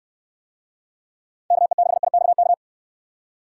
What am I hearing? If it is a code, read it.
Morse code, heard as a keyed tone, about 69 wpm, 700 Hz. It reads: KE6IXK